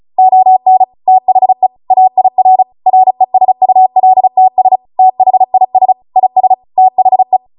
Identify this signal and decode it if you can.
Morse code, heard as a keyed tone, about 36 wpm, 750 Hz. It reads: ON THE AIR RESULTS THIS IS THE